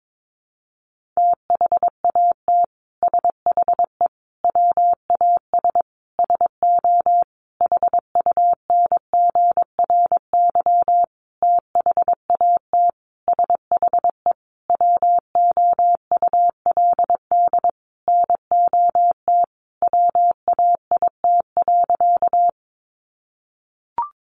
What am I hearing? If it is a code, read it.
Morse code, heard as a keyed tone, about 22 wpm, 700 Hz. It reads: THAT SHE WAS SO HUNGRY THAT SHE WOULD NOT WAIT.
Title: Sentences from Top 1400 Words 22wpm